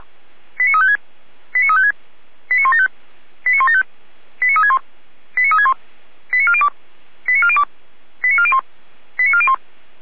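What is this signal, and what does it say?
Five-tone selective call (ZVEI), heard as quick runs of five tones, tones of about 80 ms. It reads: 89356 89356 89175 89175 89361 89361 89402 89402 89401 89401
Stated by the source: ZVEI five-tone selective call received on SDR.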